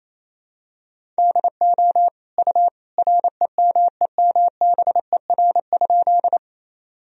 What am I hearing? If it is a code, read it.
Morse code, heard as a keyed tone, about 28 wpm, 700 Hz. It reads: DO U REMEMBER?
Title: Sentences from Top 400 Words and Abbreviations 28wpm